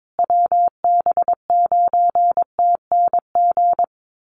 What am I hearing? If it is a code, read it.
Morse code, heard as a keyed tone, about 22 wpm, 700 Hz. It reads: WB9TNG
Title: USA Call Signs Repeated 3x 22wpm